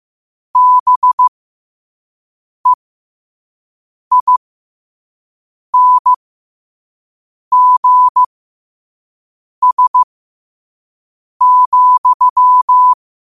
Morse code, heard as a keyed tone, about 15 wpm, spaced as for 5 wpm, 1000 Hz. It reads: BEINGS,
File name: qotdfiles / 5wpm